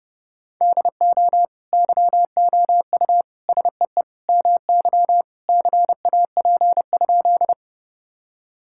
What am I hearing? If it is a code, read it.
Morse code, heard as a keyed tone, about 30 wpm, 700 Hz. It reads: DO YOU SEE MY CAP?
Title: Sentences from Top 1600 Words 30wpm